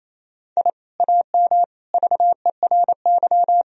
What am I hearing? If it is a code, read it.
Morse code, heard as a keyed tone, about 28 wpm, 700 Hz. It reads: I AM VERY
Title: Sentences from Top 1000 Words - Review Part 4 - 28wpm